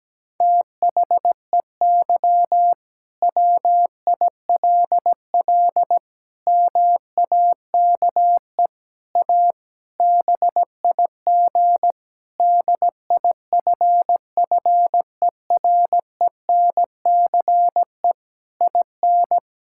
Morse code, heard as a keyed tone, about 17 wpm, 700 Hz. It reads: THEY WILL MAKE A BIG DIFFERENCE IN
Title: Sentences from Top 1000 Words - Review Part 5 - 17wpm